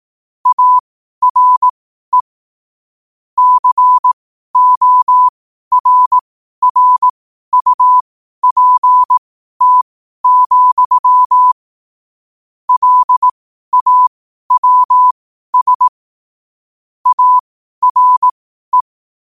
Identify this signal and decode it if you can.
Morse code, heard as a keyed tone, about 18 wpm, spaced as for 12 wpm, 1000 Hz. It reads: ARE CORRUPT, LAWS ARE
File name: qotdfiles / 12wpm